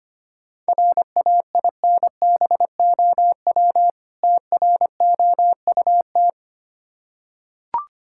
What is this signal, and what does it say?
Morse code, heard as a keyed tone, about 25 wpm, 700 Hz. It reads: RAINBOW TROUT
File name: practice / 25wpm / Difficult Words with Hints - 25wpm